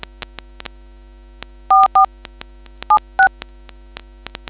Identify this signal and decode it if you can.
DTMF, heard as telephone touch tones, tones of about 120 ms. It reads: 44 76